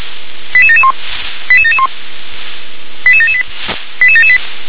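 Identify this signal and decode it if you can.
Five-tone selective call (ZVEI), heard as quick runs of five tones, tones of about 70 ms. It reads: 88771 88771 88778 88778
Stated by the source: ZVEI five-tone selective call 88920, received by SDR.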